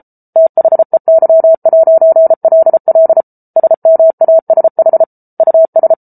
Morse code, heard as a keyed tone, about 34 wpm, 650 Hz. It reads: THEY'LL SMASH US